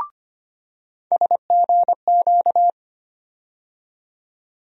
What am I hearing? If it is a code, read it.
Morse code, heard as a keyed tone, about 25 wpm, 700 Hz. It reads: SGQ
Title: USA Call Sign Postfixes - 3 Letters 25wpm